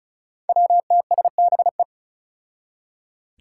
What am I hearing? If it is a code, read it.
Morse code, heard as a keyed tone, about 36 wpm, 700 Hz. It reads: WTSBE
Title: Five Challenging Letter-Number Combinations 35wpm